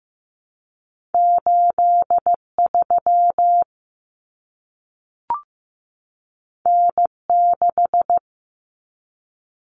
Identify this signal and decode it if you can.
Morse code, heard as a keyed tone, about 15 wpm, 700 Hz. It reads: 83 N6
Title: Common Cut Numbers - 2-Digit 15wpm